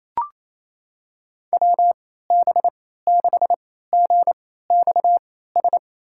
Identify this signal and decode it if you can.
Morse code, heard as a keyed tone, about 28 wpm, spaced as for 16 wpm, 700 Hz. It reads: WB6GXS
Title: USA Call Signs - 6 Characters - 3x Character Spacing - 28wpm